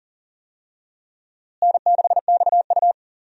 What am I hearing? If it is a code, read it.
Morse code, heard as a keyed tone, about 40 wpm, 700 Hz. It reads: N6XU
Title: USA Call Signs - 4 Characters - Repeated 3x - 40wpm